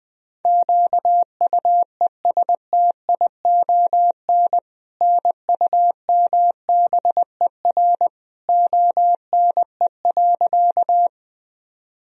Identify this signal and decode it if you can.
Morse code, heard as a keyed tone, about 20 wpm, 700 Hz. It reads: QUESTION NUMBER ONE.
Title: Sentences from Top 1000 Words - Review Part 1 - 20wpm